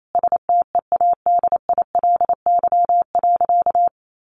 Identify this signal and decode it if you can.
Morse code, heard as a keyed tone, about 28 wpm, 700 Hz. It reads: STEADILY.
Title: A Sound Of Thunder 28wpm